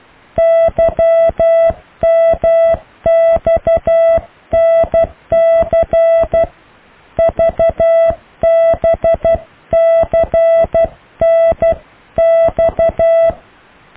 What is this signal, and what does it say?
Morse code, heard as a keyed tone, about 12 wpm, 660 Hz. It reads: YMXNC VBCNX
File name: cw / Alle letters